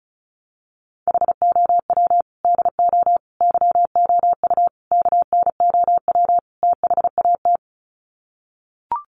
Morse code, heard as a keyed tone, about 34 wpm, 700 Hz. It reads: HOW DO YOU KNOW THAT
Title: Sets of 5 Words Encore 35wpm